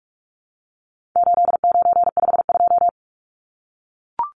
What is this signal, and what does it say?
Morse code, heard as a keyed tone, about 44 wpm, 700 Hz. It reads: 8952
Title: Numbers - 4-Digit 45wpm